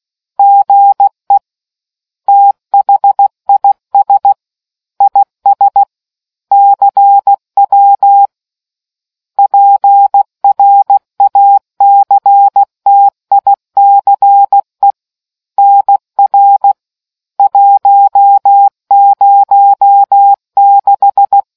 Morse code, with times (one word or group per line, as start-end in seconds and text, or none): 0.4-1.4 GE
2.3-4.3 THIS
5.0-5.8 IS
6.5-8.3 CW
9.4-14.9 PRACTICE
15.6-16.7 NR
17.4-21.4 106